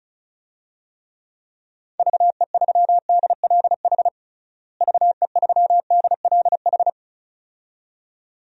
Morse code, heard as a keyed tone, about 34 wpm, 700 Hz. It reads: VE3DLH VE3DLH
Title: DXCC-100 Call Signs - 6 Characters - Repeated 2x - 35wpm